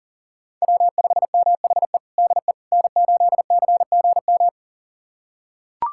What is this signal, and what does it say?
Morse code, heard as a keyed tone, about 40 wpm, 700 Hz. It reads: W5MHE DE N8CGM
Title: USA Callsign DE Callsign - 5 Characters - 40wpm